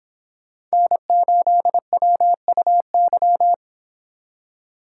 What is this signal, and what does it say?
Morse code, heard as a keyed tone, about 30 wpm, 700 Hz. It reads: N8WUY